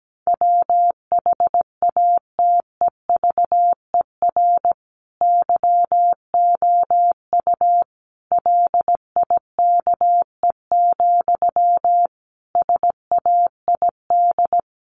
Morse code, heard as a keyed tone, about 17 wpm, 700 Hz. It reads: WHATEVER YOU LIKE, SAID
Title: Sentences from Top 1100 Words 17wpm